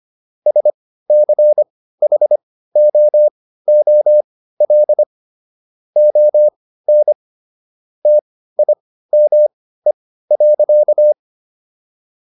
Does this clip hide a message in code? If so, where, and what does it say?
Yes: Morse code from 0.5 to 11.1 s, heard as a keyed tone, about 25 wpm, spaced as for 15 wpm, 600 Hz. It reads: SCHOOL ON TIME.